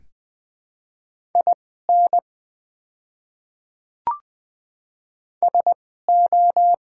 Morse code, heard as a keyed tone, about 20 wpm, spaced as for 15 wpm, 700 Hz. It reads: IN SO
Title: 2x Character Spacing - Top 100 Words - 20wpm